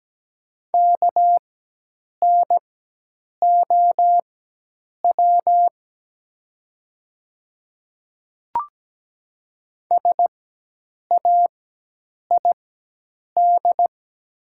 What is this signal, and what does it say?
Morse code, heard as a keyed tone, about 17 wpm, spaced as for 8 wpm, 700 Hz. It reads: KNOW SAID
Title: 4x Character Spacing - Top 100 Words - 17wpm